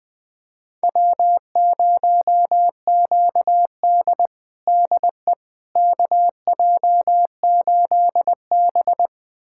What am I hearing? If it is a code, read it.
Morse code, heard as a keyed tone, about 21 wpm, 700 Hz. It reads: W0QD DE KJ8B
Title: USA Callsign DE Callsign - 4 Characters - 20wpm